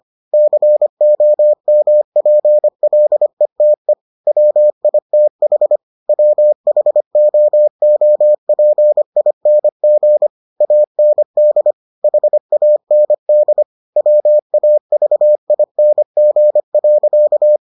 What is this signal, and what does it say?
Morse code, heard as a keyed tone, about 25 wpm, 600 Hz. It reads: COMPLETE WITH WHOOPING AND HAND WAVING.